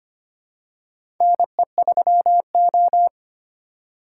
Morse code, heard as a keyed tone, about 25 wpm, 700 Hz. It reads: NE3O
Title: USA Call Signs - 4 Characters - 4x Character Spacing - 25wpm